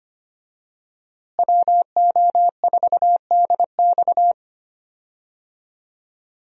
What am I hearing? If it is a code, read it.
Morse code, heard as a keyed tone, about 25 wpm, 700 Hz. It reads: WO4DX
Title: USA Call Signs - 5 Characters - Repeated 2x - 25wpm